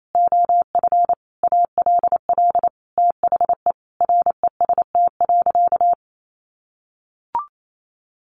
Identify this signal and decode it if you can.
Morse code, heard as a keyed tone, about 28 wpm, 700 Hz. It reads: OF ALL THE REST.
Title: Sentences from Top 1400 Words 28wpm